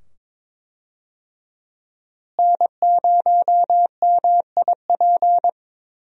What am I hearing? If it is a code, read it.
Morse code, heard as a keyed tone, about 22 wpm, 700 Hz. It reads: N0MIP
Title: USA Call Signs - 5 Characters - Repeated 2x - 22wpm